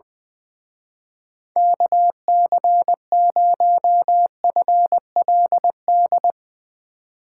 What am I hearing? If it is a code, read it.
Morse code, heard as a keyed tone, about 20 wpm, 700 Hz. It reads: KC0FLD
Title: Speed Racing - USA 2x3 Call Signs 20wpm